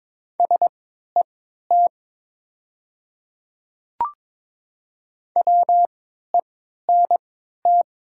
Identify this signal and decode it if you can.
Morse code, heard as a keyed tone, about 22 wpm, spaced as for 12 wpm, 700 Hz. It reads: SET WENT
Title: 3x Character Spacing - Top 300 Words - 22wpm